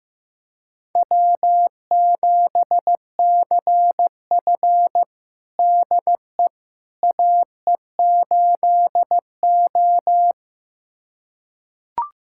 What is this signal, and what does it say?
Morse code, heard as a keyed tone, about 15 wpm, 700 Hz. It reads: W7CF DE AE8O
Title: USA Callsign DE Callsign - 4 Characters - 15wpm